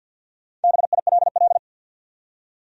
Morse code, heard as a keyed tone, about 53 wpm, 700 Hz. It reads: BILL
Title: USA Common Names - Max 5 Letters - 50wpm